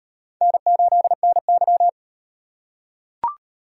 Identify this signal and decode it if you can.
Morse code, heard as a keyed tone, about 35 wpm, 700 Hz. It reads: N8NY